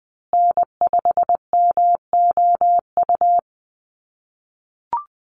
Morse code, heard as a keyed tone, about 20 wpm, 700 Hz. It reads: N5MOU
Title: USA Call Signs Repeated 3x 20wpm